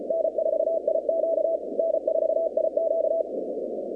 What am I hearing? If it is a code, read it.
Morse code, heard as a keyed tone, about 35 wpm, 650 Hz, with noise. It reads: N4IQ N4IQ
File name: contests / 2019-cw-wpx-kt5j-k5pi- -2330 0-47